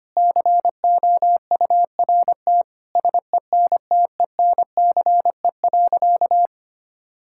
Morse code, heard as a keyed tone, about 25 wpm, 700 Hz. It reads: COURT SENTENCE.
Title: Sentences from Top 1600 Words 25wpm